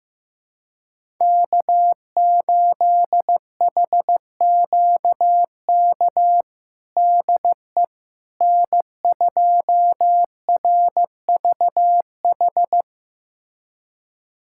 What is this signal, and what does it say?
Morse code, heard as a keyed tone, about 15 wpm, 700 Hz. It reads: K8HQK DE N2RVH
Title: USA Callsign DE Callsign - 5 Characters - 15wpm